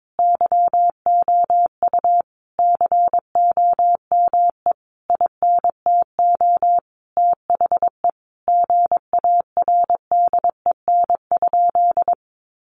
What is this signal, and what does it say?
Morse code, heard as a keyed tone, about 22 wpm, 700 Hz. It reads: YOU COME INTO THE GARDEN?
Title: Sentences from Top 700 Words 22wpm